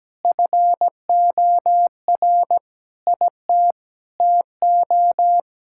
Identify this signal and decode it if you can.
Morse code, heard as a keyed tone, about 17 wpm, 700 Hz. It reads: FOR IT TO
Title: Sentences from Top 1000 Words - Review Part 5 - 17wpm